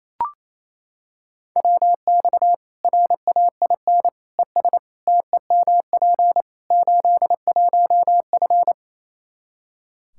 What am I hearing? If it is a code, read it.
Morse code, heard as a keyed tone, about 28 wpm, 700 Hz. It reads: WX RAIN ES TEMP 81F
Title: Practice QSO - Exchange 2 - Info - 28wpm